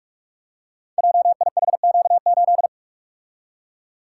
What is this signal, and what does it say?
Morse code, heard as a keyed tone, about 44 wpm, 700 Hz. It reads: JIHQ8
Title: Code Groups - 5 Letter:Numbers - 45wpm